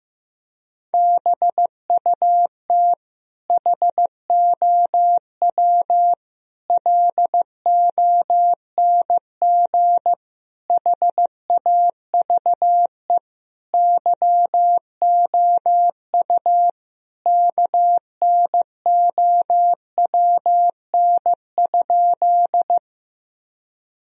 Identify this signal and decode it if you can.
Morse code, heard as a keyed tone, about 16 wpm, 700 Hz. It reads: BUT HOW LONG HAVE YOU KNOWN?
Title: Sentences from Top 1000 Words - Review Part 3 - 15wpm